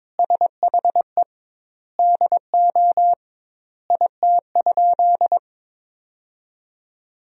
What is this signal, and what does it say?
Morse code, heard as a keyed tone, about 22 wpm, 700 Hz. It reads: SHE DO IT?
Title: 2x Word-Spacing - Sentences from Top 100 Words - 22wpm